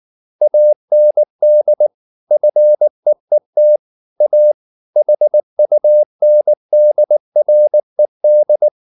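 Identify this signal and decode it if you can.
Morse code, heard as a keyed tone, about 19 wpm, 600 Hz. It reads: AND FEET A HUNDRED
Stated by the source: Morse code at 19 wpm from CW med Gnister, reading Ken Follett's novel Never in English.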